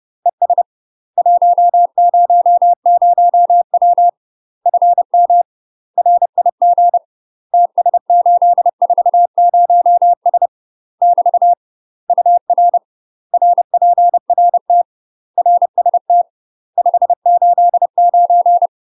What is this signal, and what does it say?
Morse code, heard as a keyed tone, about 30 wpm, 700 Hz. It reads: ES 100W FM RIG TS840S <BT> UR RPRT RST 589
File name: UK MASTER QSOs TRAINING / 30WPM / UK MASTER QSOs 0020